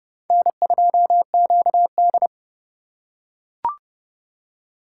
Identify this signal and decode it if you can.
Morse code, heard as a keyed tone, about 30 wpm, 700 Hz. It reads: N2QD
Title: USA Call Signs - 3x Character Spacing - 30wpm